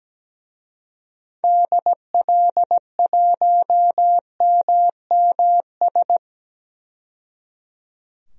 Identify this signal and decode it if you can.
Morse code, heard as a keyed tone, about 17 wpm, 700 Hz. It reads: DL1MMS
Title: DXCC-100 Call Signs - 6 Characters - 17wpm